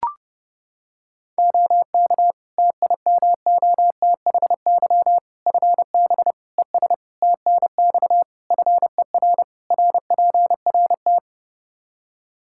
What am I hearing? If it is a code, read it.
Morse code, heard as a keyed tone, about 32 wpm, 700 Hz. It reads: OK TIMOTHY FB ES TNX FER RPRT
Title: K1USN SST QSO Practice 30wpm